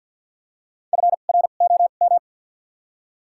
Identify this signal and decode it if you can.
Morse code, heard as a keyed tone, about 51 wpm, spaced as for 36 wpm, 700 Hz. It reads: FROM
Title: 2x Character Spacing - Top 100 Words - 50wpm